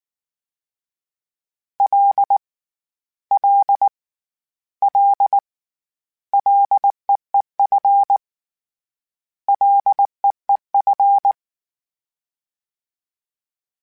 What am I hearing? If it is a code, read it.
Morse code, heard as a keyed tone, about 19 wpm, 800 Hz. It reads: L L L LEEF LEEF
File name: Short Runs / Short Runs-mp3s / SinglePrefixes @18